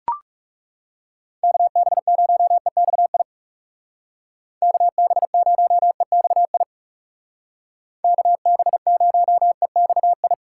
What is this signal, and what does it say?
Morse code, heard as a keyed tone, about 40 wpm, 700 Hz. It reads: KB0EXI KB0EXI KB0EXI